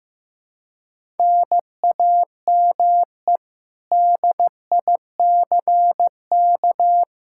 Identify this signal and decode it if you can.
Morse code, heard as a keyed tone, about 15 wpm, 700 Hz. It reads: NAME DICK
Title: Practice QSO - Exchange 1 - Beginning - 15wpm